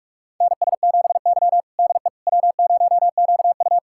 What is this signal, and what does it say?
Morse code, heard as a keyed tone, about 44 wpm, 700 Hz. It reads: NS7Y DE W0QU
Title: USA Callsign DE Callsign - 4 Characters - 45wpm